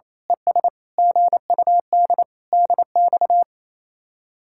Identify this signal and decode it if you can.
Morse code, heard as a keyed tone, about 27 wpm, 700 Hz. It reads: ES GUD DX
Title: Speed-Racing Common QSO Phrases - 28wpm